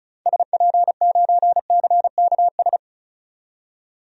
Morse code, heard as a keyed tone, about 36 wpm, 700 Hz. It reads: SP9CKS
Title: DXCC-100 Call Signs - 6 Characters - 35wpm